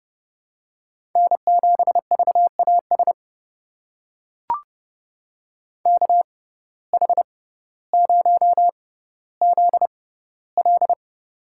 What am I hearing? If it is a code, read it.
Morse code, heard as a keyed tone, about 29 wpm, spaced as for 15 wpm, 700 Hz. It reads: N7VAS KH0ZL
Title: USA Call Signs - 5 Characters - 6x Character Spacing - 30wpm